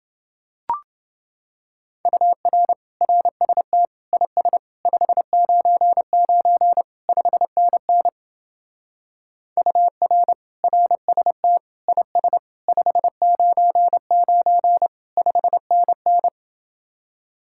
Morse code, heard as a keyed tone, about 27 wpm, 700 Hz. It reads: UR RST IS 599 5NN UR RST IS 599 5NN